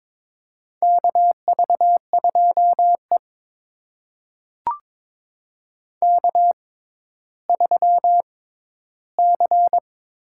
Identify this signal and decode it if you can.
Morse code, heard as a keyed tone, about 22 wpm, spaced as for 12 wpm, 700 Hz. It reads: KV2E K3C
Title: USA Call Signs - 4 Characters - 6x Character Spacing - 22wpm